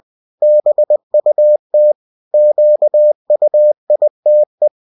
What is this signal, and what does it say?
Morse code, heard as a keyed tone, about 20 wpm, 600 Hz. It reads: BUT QUITE
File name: cw-books / winnie-the-pooh / winnie-20wpm-page-0023